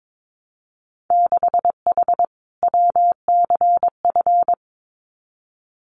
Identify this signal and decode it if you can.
Morse code, heard as a keyed tone, about 22 wpm, 700 Hz. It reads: 6H WCF
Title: Winter Field Day Exchanges Encore 22wpm